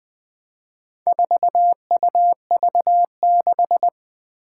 Morse code, heard as a keyed tone, about 20 wpm, 700 Hz. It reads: 4UV6
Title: Four Challenging Letter-Number Combinations 20wpm